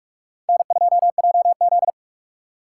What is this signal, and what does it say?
Morse code, heard as a keyed tone, about 39 wpm, 700 Hz. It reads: N2JZ